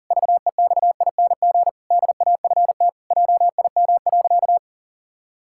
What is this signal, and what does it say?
Morse code, heard as a keyed tone, about 40 wpm, 700 Hz. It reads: VEXING DAFT JIM.
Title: Pangram Challenge v2 - 40wpm